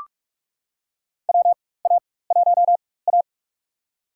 Morse code, heard as a keyed tone, about 46 wpm, spaced as for 21 wpm, 700 Hz. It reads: WA1A
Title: USA Call Signs - 4 Characters - 4x Character Spacing - 45wpm